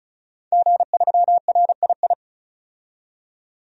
Morse code, heard as a keyed tone, about 36 wpm, 700 Hz. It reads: G3RII
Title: DXCC-100 Call Signs - 5 Characters - 35wpm